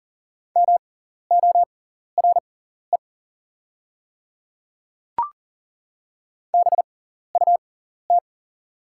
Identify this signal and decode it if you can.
Morse code, heard as a keyed tone, about 39 wpm, spaced as for 14 wpm, 700 Hz. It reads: MORE BUT
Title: 6x Character Spacing - Top 200 Words - 40wpm